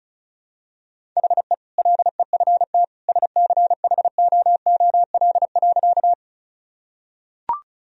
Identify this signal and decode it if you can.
Morse code, heard as a keyed tone, about 34 wpm, 700 Hz. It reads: HE LEFT SCHOOL.
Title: Sentences from Top 1000 Words - Review Part 1 - 35wpm